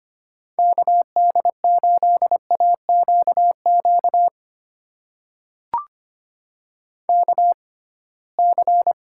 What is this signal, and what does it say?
Morse code, heard as a keyed tone, about 25 wpm, spaced as for 19 wpm, 700 Hz. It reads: KD8AQQ KC
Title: USA Call Signs - 6 Characters - 6x Character Spacing - 25wpm